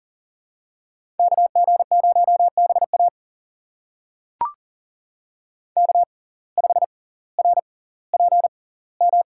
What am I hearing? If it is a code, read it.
Morse code, heard as a keyed tone, about 40 wpm, spaced as for 20 wpm, 700 Hz. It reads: KG0BA K5RPM
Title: USA Call Signs - 5 Characters - 6x Character Spacing - 40wpm